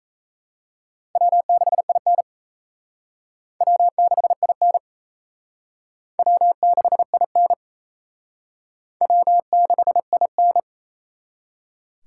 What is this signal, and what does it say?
Morse code, heard as a keyed tone, about 34 wpm, 700 Hz. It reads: W6IN W6IN W6IN W6IN
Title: Speed Racing - USA 1x2 Call Signs 28wpm